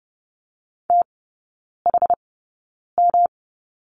Morse code, heard as a keyed tone, about 30 wpm, spaced as for 9 wpm, 700 Hz. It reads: THM
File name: Lesson 18e - Letter m - Three Characters 30wpm